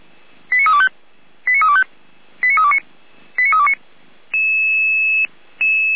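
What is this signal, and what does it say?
Five-tone selective call (ZVEI), heard as quick runs of five tones, tones of about 70 ms. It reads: 89325 89325 89329 89329
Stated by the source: ZVEI five-tone selective call received on an SDR.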